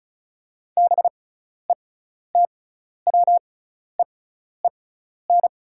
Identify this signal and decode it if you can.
Morse code, heard as a keyed tone, about 36 wpm, spaced as for 12 wpm, 700 Hz. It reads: BETWEEN
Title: 6x Character Spacing - Top 300 Words - 35wpm